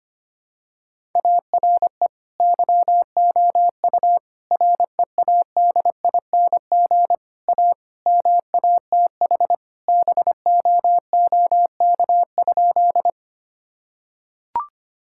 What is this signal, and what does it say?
Morse code, heard as a keyed tone, about 25 wpm, 700 Hz. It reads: ARE YOU READING A MATH BOOK?
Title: Sentences from Top 1000 Words - Review Part 6 - 25wpm